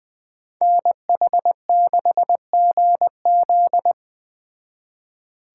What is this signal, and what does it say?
Morse code, heard as a keyed tone, about 20 wpm, 700 Hz. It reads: NH6GZ
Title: Speed Racing - USA 2x2 Call Signs 17wpm